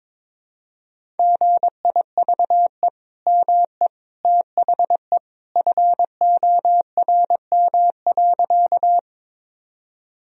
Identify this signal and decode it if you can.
Morse code, heard as a keyed tone, about 22 wpm, 700 Hz. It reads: GIVE ME THE FORM.
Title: Sentences from Top 1000 Words - Review Part 1 - 22wpm